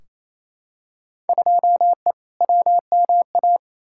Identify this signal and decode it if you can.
Morse code, heard as a keyed tone, about 28 wpm, 700 Hz. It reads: 2E WMA
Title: Field Day Exchanges Encore - 28wpm